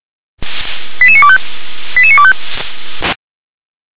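Five-tone selective call (ZVEI), heard as quick runs of five tones, tones of about 70 ms. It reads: 88925 88925
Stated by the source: ZVEI five-tone call received by SDR.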